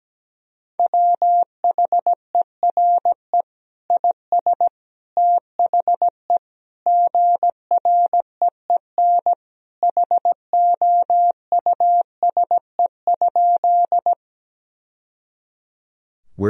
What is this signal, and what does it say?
Morse code, heard as a keyed tone, about 17 wpm, 700 Hz. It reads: WHERE IS THE GREEN HOUSE?
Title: Sentences from Top 1000 Words - Review Part 2 - 17wpm